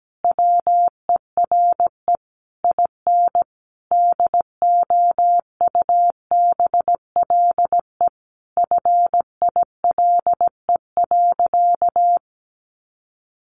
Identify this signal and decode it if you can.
Morse code, heard as a keyed tone, about 17 wpm, 700 Hz. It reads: WERE IN DOUBLE FILE.
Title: Sentences from Top 1400 Words 17wpm